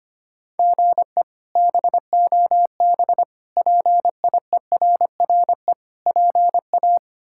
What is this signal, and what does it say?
Morse code, heard as a keyed tone, about 25 wpm, 700 Hz. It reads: GE BOB PIERRE PA
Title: K1USN SST QSO Practice - 25wpm